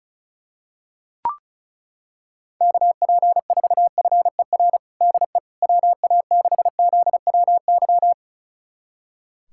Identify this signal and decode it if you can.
Morse code, heard as a keyed tone, about 36 wpm, 700 Hz. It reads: KP4FER DE WA6ZWY
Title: USA Callsign DE Callsign - 6 Characters - 35wpm